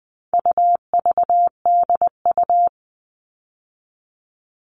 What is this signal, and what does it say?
Morse code, heard as a keyed tone, about 20 wpm, 700 Hz. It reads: UVDU
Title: All Cut Numbers - 4-Digit 20wpm